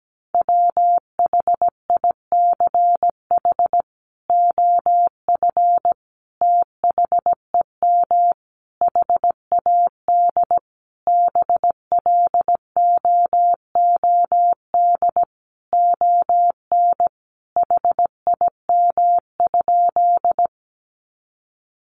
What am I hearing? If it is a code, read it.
Morse code, heard as a keyed tone, about 17 wpm, 700 Hz. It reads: WHICH OF THEM HAD BLOOD ON HIM?
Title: Sentences from Top 1100 Words 17wpm